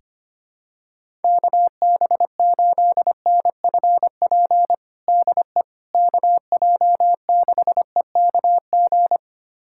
Morse code, heard as a keyed tone, about 25 wpm, 700 Hz. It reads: KB8NFP DE KJ6EKG
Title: USA Callsign DE Callsign - 6 Characters - 25wpm